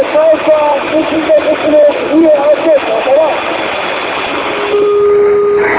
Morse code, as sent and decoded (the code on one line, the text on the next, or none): none